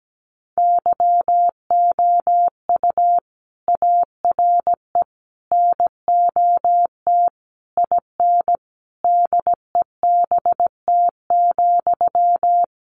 Morse code, heard as a keyed tone, about 17 wpm, 700 Hz. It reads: YOU ARE NOT IN DEBT,
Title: Sentences from Top 1300 Words 17wpm